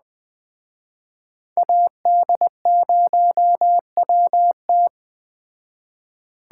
Morse code, heard as a keyed tone, about 20 wpm, 700 Hz. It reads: AD0WT
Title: Speed Racing - USA 2x2 Call Signs 17wpm